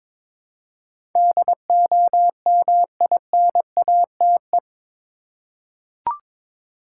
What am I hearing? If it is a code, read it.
Morse code, heard as a keyed tone, about 22 wpm, 700 Hz. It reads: DOMINATE
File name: practice / 22wpm / Eight Letter Words - 22wpm